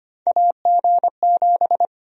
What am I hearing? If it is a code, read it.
Morse code, heard as a keyed tone, about 25 wpm, 700 Hz. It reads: AG7
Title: USA Call Sign Prefixes - 25wpm